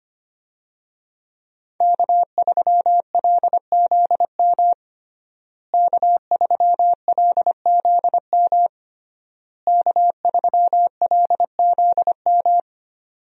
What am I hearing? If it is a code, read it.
Morse code, heard as a keyed tone, about 25 wpm, 700 Hz. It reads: K3LZM K3LZM K3LZM
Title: USA Call Signs - 5 Characters - Repeated 3x - 25wpm